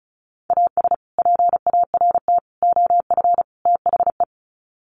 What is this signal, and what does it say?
Morse code, heard as a keyed tone, about 34 wpm, 700 Hz. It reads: AS PART OF THE
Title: Sets of 4 Words 35wpm